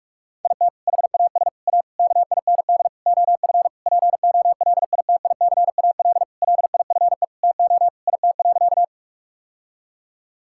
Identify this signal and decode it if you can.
Morse code, heard as a keyed tone, about 46 wpm, 700 Hz. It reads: IT HAS A KIND OF POLITICAL LIFE TO IT.
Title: Sentences from Top 1000 Words - Review Part 5 - 45wpm